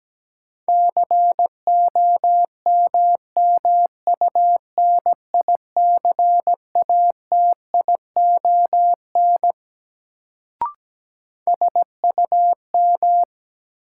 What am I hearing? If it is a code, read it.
Morse code, heard as a keyed tone, about 17 wpm, 700 Hz. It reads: COMMUNICATION SUM
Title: Sentences from Top 1000 Words - Review Part 6 - New Word Drill - 17wpm